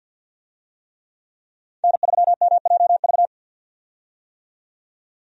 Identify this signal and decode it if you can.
Morse code, heard as a keyed tone, about 47 wpm, 700 Hz. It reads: N3MJV
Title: USA Call Signs - 5 Characters - 50wpm